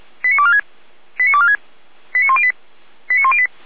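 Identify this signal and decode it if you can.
Five-tone selective call (ZVEI), heard as quick runs of five tones, tones of about 70 ms. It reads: 89246 89246 89108 89108
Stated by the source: ZVEI five-tone selective call 89560 received on SDR.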